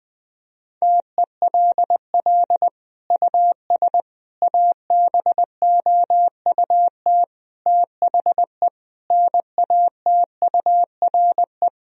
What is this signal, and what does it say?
Morse code, heard as a keyed tone, about 21 wpm, 700 Hz. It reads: TELL US ABOUT THE NATURE
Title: Sentences from Top 1000 Words - Review Part 4 - 20wpm